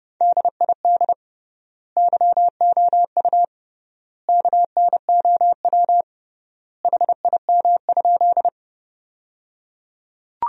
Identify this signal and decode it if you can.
Morse code, heard as a keyed tone, about 30 wpm, 700 Hz. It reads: DID YOU KNOW HIM?
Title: 3x Word-Spacing - Sentences from Top 100 Words - 30wpm